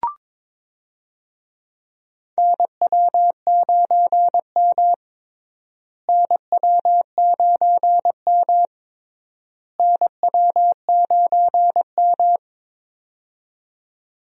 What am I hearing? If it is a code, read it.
Morse code, heard as a keyed tone, about 22 wpm, 700 Hz. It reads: NW9M NW9M NW9M
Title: USA Call Signs - 4 Characters - Repeated 3x - 22wpm